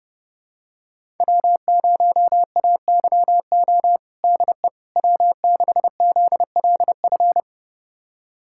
Morse code, heard as a keyed tone, about 31 wpm, 700 Hz. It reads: W0AYO DE W6ZLF
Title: USA Callsign DE Callsign - 5 Characters - 30wpm